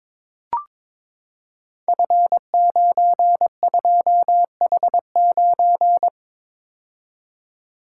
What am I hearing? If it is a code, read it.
Morse code, heard as a keyed tone, about 22 wpm, 700 Hz. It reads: F92H9
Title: Code Groups - 5 Letter:Numbers - 22wpm